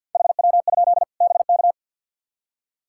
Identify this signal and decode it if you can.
Morse code, heard as a keyed tone, about 53 wpm, 700 Hz. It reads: HW? BK
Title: Practice QSO - Exchange 3 - More Info - 50wpm